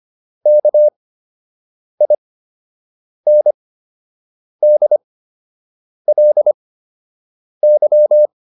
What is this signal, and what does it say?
Morse code, heard as a keyed tone, about 25 wpm, spaced as for 7 wpm, 600 Hz. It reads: KINDLY